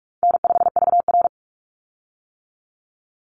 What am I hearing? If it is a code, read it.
Morse code, heard as a keyed tone, about 46 wpm, 700 Hz. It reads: N5VR
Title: Easy Call Signs - 45wpm